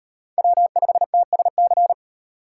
Morse code, heard as a keyed tone, about 36 wpm, 700 Hz. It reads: W5TSC